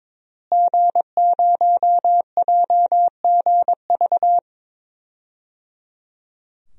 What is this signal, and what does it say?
Morse code, heard as a keyed tone, about 22 wpm, 700 Hz. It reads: G0JGV
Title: DXCC-100 Call Signs - 5 Characters - Repeated 3x - 22wpm